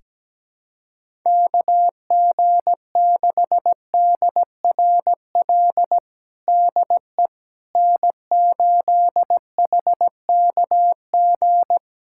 Morse code, heard as a keyed tone, about 17 wpm, 700 Hz. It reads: KG6DRL DE N8HKG
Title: USA Callsign DE Callsign 17wpm